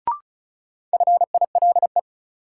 Morse code, heard as a keyed tone, about 36 wpm, 700 Hz. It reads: FILE
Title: Sentences from Top 1000 Words - Review Part 4 - New Word Drill - 35wpm